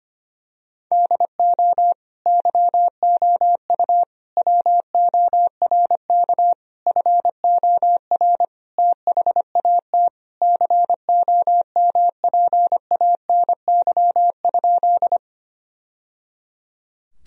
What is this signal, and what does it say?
Morse code, heard as a keyed tone, about 25 wpm, 700 Hz. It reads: DO YOU WORK FOR THAT COMPANY?
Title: Sentences from Top 1000 Words - Review Part 3 - 25wpm